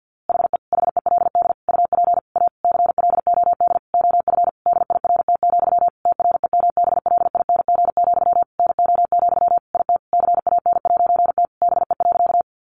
Morse code, heard as a keyed tone, about 50 wpm, 700 Hz. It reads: HE HELD UP A CLOD OF DIRT, TREMBLING, NO, IT CAN'T BE.
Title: A Sound Of Thunder 50wpm